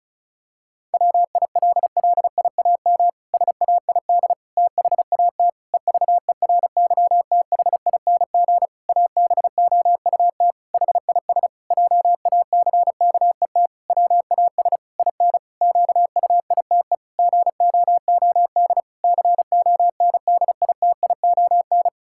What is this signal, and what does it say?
Morse code, heard as a keyed tone, about 36 wpm, 700 Hz. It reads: WILLIAM SAID THAT EVERYTHING ABOUT HIS JACKET WAS IN QUITE GOOD CONDITION
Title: Pangram Challenge v2 - 35wpm